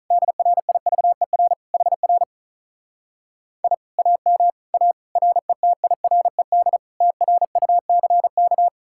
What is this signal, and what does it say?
Morse code, heard as a keyed tone, about 47 wpm, 700 Hz. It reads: DRIVER HR I AM A RETIRED TRUCK